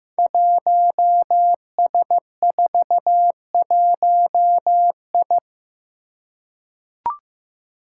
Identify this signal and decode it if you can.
Morse code, heard as a keyed tone, about 15 wpm, 700 Hz. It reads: 1S41I
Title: Five Challenging Letter-Number Combinations 15wpm